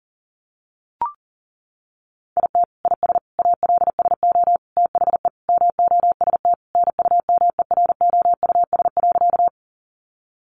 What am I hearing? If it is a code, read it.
Morse code, heard as a keyed tone, about 40 wpm, 700 Hz. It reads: IT IS ALSO THE MOST NUMEROUS.
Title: Sentences from Top 1400 Words 40wpm